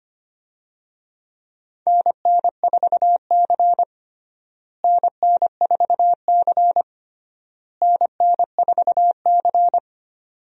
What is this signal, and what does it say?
Morse code, heard as a keyed tone, about 25 wpm, 700 Hz. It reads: NN4C NN4C NN4C
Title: USA Call Signs - 4 Characters - Repeated 3x - 25wpm